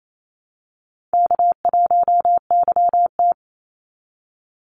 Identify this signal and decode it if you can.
Morse code, heard as a keyed tone, about 28 wpm, 700 Hz. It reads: K1YT
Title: Easy Call Signs Repeated 3x - 28wpm